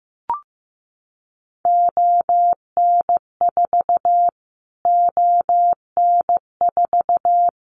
Morse code, heard as a keyed tone, about 15 wpm, 700 Hz. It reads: ON4 ON4
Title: Callsign Prefixes for Top 25 DXCC - Repeated 3x - 15wpm